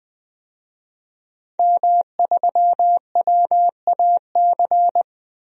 Morse code, heard as a keyed tone, about 20 wpm, 700 Hz. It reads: M3WAC
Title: DXCC-100 Call Signs - 5 Characters - 20wpm